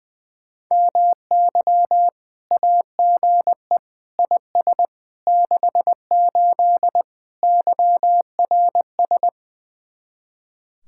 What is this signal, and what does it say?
Morse code, heard as a keyed tone, about 20 wpm, 700 Hz. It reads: MY AGE IS 68 YRS
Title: Speed-Racing Common QSO Phrases - 20wpm